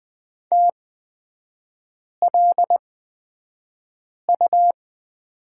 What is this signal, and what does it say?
Morse code, heard as a keyed tone, about 20 wpm, spaced as for 5 wpm, 700 Hz. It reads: TLU
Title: 250 Lesson 40e - BK - Three Characters 20wpm